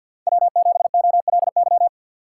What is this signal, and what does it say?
Morse code, heard as a keyed tone, about 47 wpm, 700 Hz. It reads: W7OLY
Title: USA Call Signs - 5 Characters - 4x Character Spacing - 50wpm